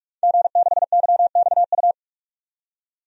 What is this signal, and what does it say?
Morse code, heard as a keyed tone, about 46 wpm, 700 Hz. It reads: G6YXU
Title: DXCC-100 Call Signs - 5 Characters - 45wpm